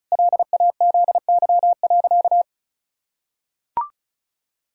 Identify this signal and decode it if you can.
Morse code, heard as a keyed tone, about 36 wpm, 700 Hz. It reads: LAZY.
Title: Pangram Challenge v2 - 35wpm